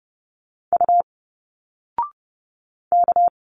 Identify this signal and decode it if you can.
Morse code, heard as a keyed tone, about 30 wpm, 700 Hz. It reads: U K
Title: Single Letters 30wpm